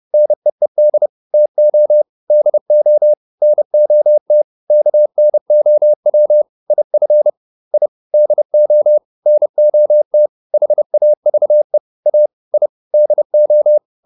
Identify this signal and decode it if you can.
Morse code, heard as a keyed tone, about 30 wpm, 600 Hz. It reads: NEED TO DO NOT KNOW IF I DO NOT HAVE A I DO